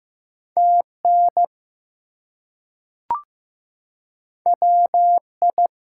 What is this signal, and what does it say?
Morse code, heard as a keyed tone, about 15 wpm, 700 Hz. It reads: TN WI
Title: US States and Canadian Provinces 15wpm